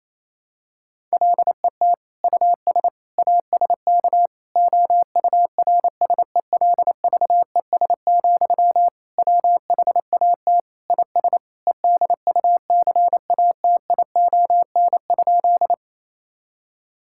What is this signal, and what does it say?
Morse code, heard as a keyed tone, about 28 wpm, 700 Hz. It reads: LET US ASK OURSELVES, WHAT IS EDUCATION?
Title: Sentences from Top 1600 Words 28wpm